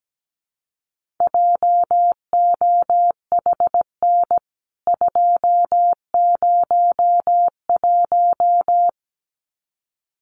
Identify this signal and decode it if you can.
Morse code, heard as a keyed tone, about 17 wpm, 700 Hz. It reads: JOHN 201
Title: CWT Exchange 17wpm